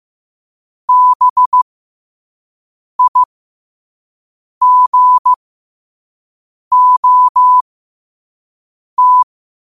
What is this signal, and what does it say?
Morse code, heard as a keyed tone, about 15 wpm, spaced as for 5 wpm, 1000 Hz. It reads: BIGOT
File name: qotdfiles / 5wpm